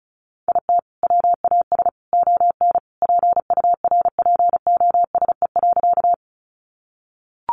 Morse code, heard as a keyed tone, about 34 wpm, 700 Hz. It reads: IT WAS ON PURPOSE.
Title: Sentences from Top 800 Words 35wpm